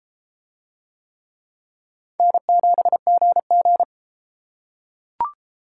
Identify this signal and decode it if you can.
Morse code, heard as a keyed tone, about 33 wpm, 700 Hz. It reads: N7GG